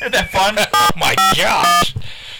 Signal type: DTMF